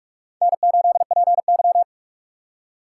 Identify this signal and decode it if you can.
Morse code, heard as a keyed tone, about 42 wpm, 690 Hz. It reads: N8PY
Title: USA Call Signs - 4 Characters - 3x Character Spacing - 45wpm